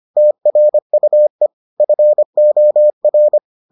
Morse code, heard as a keyed tone, about 25 wpm, 600 Hz. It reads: TRUE FOR